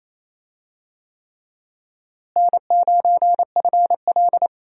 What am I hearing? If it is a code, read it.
Morse code, heard as a keyed tone, about 28 wpm, 700 Hz. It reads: N9FL
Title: USA Call Signs - 4 Characters - Repeated 3x - 28wpm